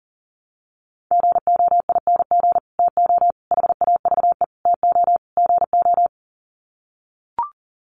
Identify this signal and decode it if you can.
Morse code, heard as a keyed tone, about 38 wpm, 700 Hz. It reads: GOING TO HAVE TO GO
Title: Sets of 5 Words Encore 40wpm